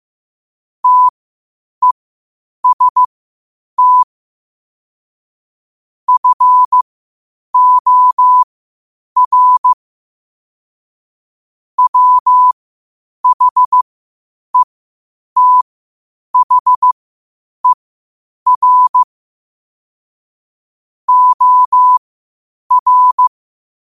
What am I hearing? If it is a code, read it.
Morse code, heard as a keyed tone, about 15 wpm, spaced as for 8 wpm, 1000 Hz. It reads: TEST FOR WHETHER OR
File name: qotdfiles / 8wpm